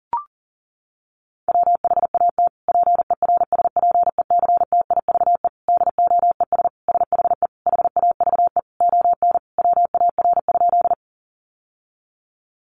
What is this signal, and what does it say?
Morse code, heard as a keyed tone, about 40 wpm, 700 Hz. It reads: WHAT PERSPECTIVE DOES SHE HAVE ON WAR?
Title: Sentences from Top 1200 Words 40wpm